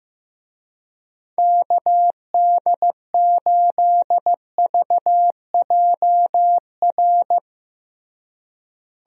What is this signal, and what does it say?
Morse code, heard as a keyed tone, about 15 wpm, 700 Hz. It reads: KD8VJR
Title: USA Call Signs - 6 Characters - Repeated 2x - 15wpm